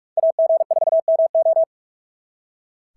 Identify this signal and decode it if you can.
Morse code, heard as a keyed tone, about 44 wpm, 650 Hz. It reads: AG4MO
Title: SOTA QSO Practice 45wpm